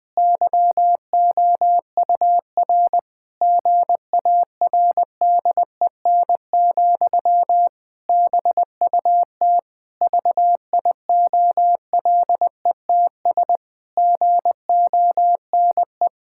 Morse code, heard as a keyed tone, about 20 wpm, 700 Hz. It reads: YOUR GARDEN, BUT VIOLETS GONE
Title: Pangram Challenge v2 - 20wpm